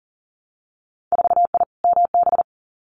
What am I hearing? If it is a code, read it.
Morse code, heard as a keyed tone, about 40 wpm, 700 Hz. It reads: 4I MB
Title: Winter Field Day Exchanges Encore 40wpm